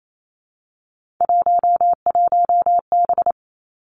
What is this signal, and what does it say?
Morse code, heard as a keyed tone, about 28 wpm, 700 Hz. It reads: 11B
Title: Three Challenging Letter-Number Combinations 28wpm